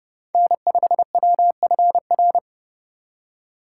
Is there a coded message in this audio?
Yes: Morse code from 0.3 to 2.4 s, heard as a keyed tone, about 29 wpm, 700 Hz. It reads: N5WFR